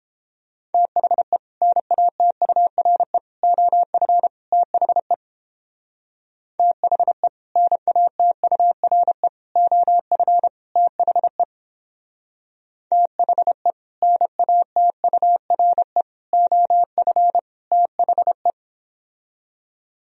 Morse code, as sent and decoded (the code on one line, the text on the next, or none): - .... / -. .- - ..- .-. / --- ..-. / - .... / - .... / -. .- - ..- .-. / --- ..-. / - .... / - .... / -. .- - ..- .-. / --- ..-. / - .... .
THE NATURE OF THE THE NATURE OF THE THE NATURE OF THE